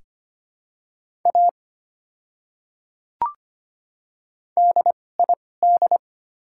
Morse code, heard as a keyed tone, about 25 wpm, spaced as for 18 wpm, 700 Hz. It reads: A DID
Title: 2x Character Spacing - Top 100 Words - 25wpm